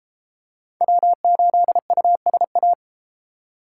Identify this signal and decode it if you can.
Morse code, heard as a keyed tone, about 30 wpm, 700 Hz. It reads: W8USA